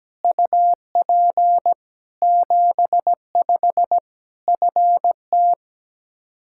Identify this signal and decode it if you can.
Morse code, heard as a keyed tone, about 17 wpm, 700 Hz. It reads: UP 75 FT
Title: Practice QSO - Exchange 2 - Info - 17wpm